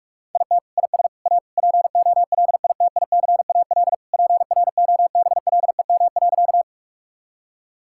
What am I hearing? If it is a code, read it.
Morse code, heard as a keyed tone, about 41 wpm, 700 Hz. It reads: IT IS A POLITICAL PROBLEM.
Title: Sentences from Top 1000 Words - Review Part 5 - 45wpm